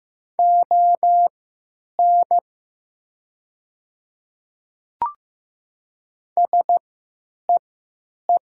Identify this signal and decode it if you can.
Morse code, heard as a keyed tone, about 15 wpm, spaced as for 9 wpm, 700 Hz. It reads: ON SEE
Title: 3x Character Spacing - Top 500 Words - 15wpm